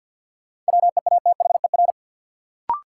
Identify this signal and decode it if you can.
Morse code, heard as a keyed tone, about 49 wpm, 700 Hz. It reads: WEATHER
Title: Sentences from Top 800 Words - New Word Drill - 50wpm